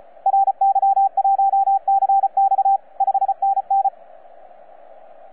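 Morse code, heard as a keyed tone, about 36 wpm, 780 Hz, with radio noise. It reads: RY1CX 5NN
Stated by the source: FS/LJ2IJ heard on 12 metres CW, from DX World.net.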